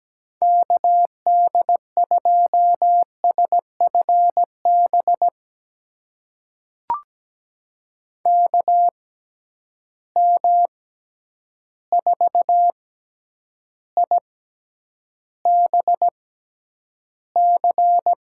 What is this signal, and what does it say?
Morse code, heard as a keyed tone, about 17 wpm, spaced as for 9 wpm, 700 Hz. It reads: KD2SFB KM4IBC
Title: USA Call Signs - 6 Characters - 6x Character Spacing - 17wpm